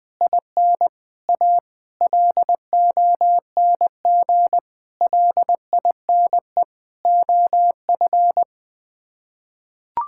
Morse code, heard as a keyed tone, about 21 wpm, 700 Hz. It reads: IN A LONG LINE OF
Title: Sets of 8 Words - 20wpm